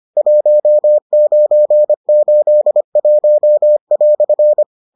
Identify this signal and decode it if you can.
Morse code, heard as a keyed tone, about 25 wpm, 600 Hz. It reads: 1981"